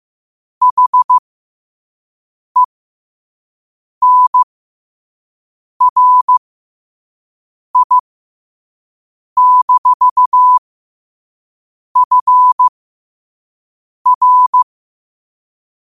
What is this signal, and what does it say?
Morse code, heard as a keyed tone, about 15 wpm, spaced as for 5 wpm, 1000 Hz. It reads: HENRI-FR
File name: qotdfiles / 5wpm